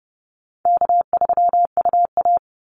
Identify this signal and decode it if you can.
Morse code, heard as a keyed tone, about 30 wpm, 700 Hz. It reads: K3UA
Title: Easy Call Signs - 30wpm